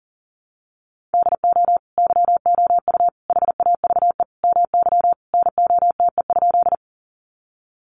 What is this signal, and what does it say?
Morse code, heard as a keyed tone, about 40 wpm, 700 Hz. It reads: DO YOU HAVE MY NOTE?
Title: Sentences from Top 500 Words 40wpm